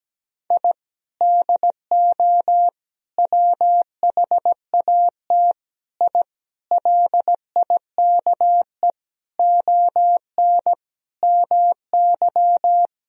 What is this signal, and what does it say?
Morse code, heard as a keyed tone, about 17 wpm, 700 Hz. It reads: I DO WHAT I LIKE ON MY
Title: Sentences from Top 1000 Words - Review Part 2 - 17wpm